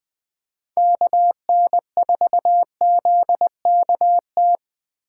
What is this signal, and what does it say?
Morse code, heard as a keyed tone, about 20 wpm, 700 Hz. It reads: KN4ZKT
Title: K1USN SST QSO Practice - 20wpm